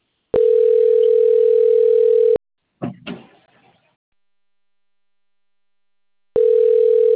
Ringback tone (North America), 440 + 480 Hz.